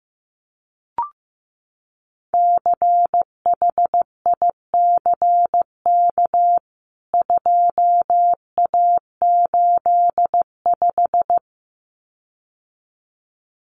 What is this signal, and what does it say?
Morse code, heard as a keyed tone, about 15 wpm, 700 Hz. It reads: CHICK 2A85
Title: CWT Exchange 15wpm